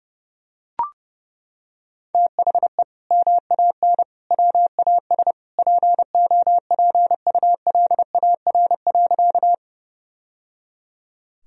Morse code, heard as a keyed tone, about 30 wpm, 700 Hz. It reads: THE MAN WAS POPULAR.
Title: Sentences from Top 1000 Words - Review Part 3 - 30wpm